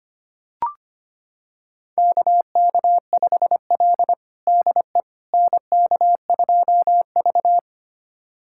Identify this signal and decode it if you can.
Morse code, heard as a keyed tone, about 25 wpm, 700 Hz. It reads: KK5L DE NK2V
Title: USA Callsign DE Callsign - 4 Characters - 25wpm